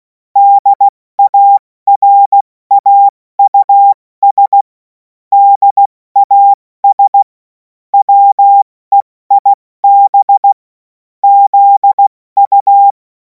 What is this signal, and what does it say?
Morse code, heard as a keyed tone, about 16 wpm, 800 Hz. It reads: DARAUS DAS WEIB ZU
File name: Morsekurs / Witze / Witz 24. Februar 2026 70 F80